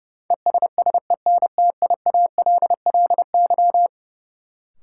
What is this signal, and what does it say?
Morse code, heard as a keyed tone, about 32 wpm, 700 Hz. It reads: ESSENTIALLY
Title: Sentences from Top 1300 Words - New Word Drill Alt - 30wpm